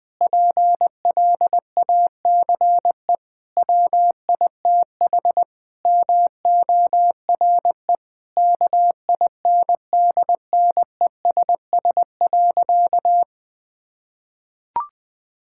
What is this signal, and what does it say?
Morse code, heard as a keyed tone, about 21 wpm, 700 Hz. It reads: PLACE WITH MORE KINDNESS.
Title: Sentences with 7-Word Phrases - Spoken Beginning - 20wpm